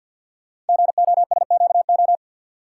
Morse code, heard as a keyed tone, about 48 wpm, 700 Hz. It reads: GOSQO
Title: Code Groups - 5 Letters - 50wpm